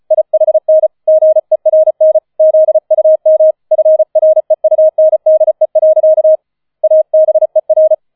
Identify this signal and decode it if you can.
Morse code, heard as a keyed tone, about 35 wpm, 630 Hz. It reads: IHN GERN ZUM FREUNDE. ABER